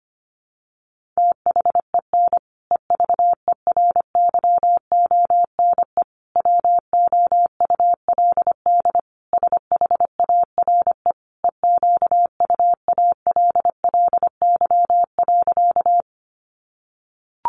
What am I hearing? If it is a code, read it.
Morse code, heard as a keyed tone, about 25 wpm, 700 Hz. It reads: THEN EVERYONE WOULD SHARE EQUALLY.
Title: Sentences from Top 1400 Words 25wpm